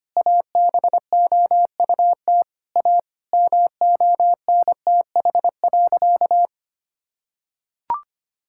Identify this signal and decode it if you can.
Morse code, heard as a keyed tone, about 25 wpm, 700 Hz. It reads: ABOUT A MONTH.
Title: Sentences from Top 1000 Words - Review Part 6 - 25wpm